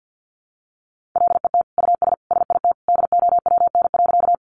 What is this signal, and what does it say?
Morse code, heard as a keyed tone, about 50 wpm, 700 Hz. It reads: LET US SIT DOWN.
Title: Sentences from Top 500 Words 50wpm